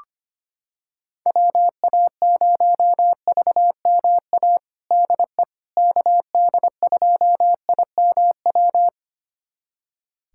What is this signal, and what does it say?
Morse code, heard as a keyed tone, about 25 wpm, 700 Hz. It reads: WA0VMA DE KD2IMW
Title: USA Callsign DE Callsign 25wpm